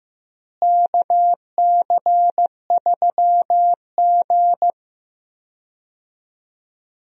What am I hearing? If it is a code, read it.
Morse code, heard as a keyed tone, about 15 wpm, 700 Hz. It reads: KC3G
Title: USA Call Signs - 4 Characters - Repeated 3x - 15wpm